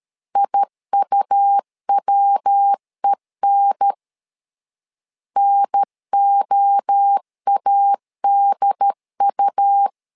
Morse code, heard as a keyed tone, about 13 wpm, 800 Hz. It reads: IUWEN NOADU
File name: LEZIONI / 3 MWRO / 3 MWRO 12wpm 1730